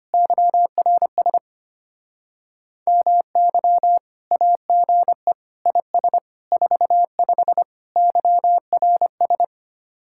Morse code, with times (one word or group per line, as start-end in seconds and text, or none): 0.1-1.4 YRS
2.9-4.0 MY
4.3-5.3 AGE
5.7-6.2 IS
6.5-7.6 45
8.0-9.4 YRS